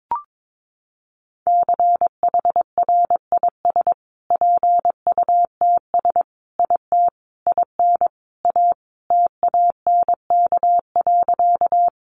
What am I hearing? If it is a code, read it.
Morse code, heard as a keyed tone, about 22 wpm, 700 Hz. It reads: CHRIS PUTS IT IN A TANK.
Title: Additional Easy Phrases 22wpm